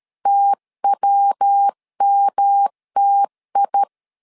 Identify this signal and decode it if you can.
Morse code, heard as a keyed tone, about 13 wpm, 800 Hz. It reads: TWMTI
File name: LEZIONI / 3 MWRO / 3 MWR 12wpm 1711